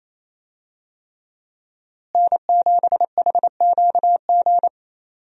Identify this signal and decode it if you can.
Morse code, heard as a keyed tone, about 28 wpm, 700 Hz. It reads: N7HQG
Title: USA Call Signs - 5 Characters - Repeated 3x - 28wpm